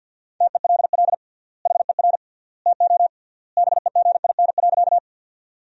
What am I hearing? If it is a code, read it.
Morse code, heard as a keyed tone, about 50 wpm, 700 Hz. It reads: TELL HER TO BEGIN.
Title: 3x Word-Spacing - Sentences from Top 300 Words - 50wpm